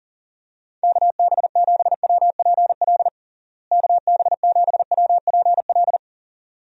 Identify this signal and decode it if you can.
Morse code, heard as a keyed tone, about 40 wpm, 700 Hz. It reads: KB7WPL KB7WPL
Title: USA Call Signs - 6 Characters - Repeated 2x - 40wpm